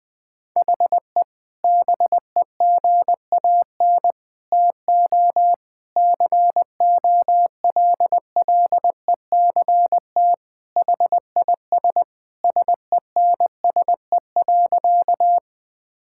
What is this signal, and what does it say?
Morse code, heard as a keyed tone, about 21 wpm, 700 Hz. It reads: HE BEGAN TO COLLECT HIS SENSE.
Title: Sentences from Top 1600 Words 20wpm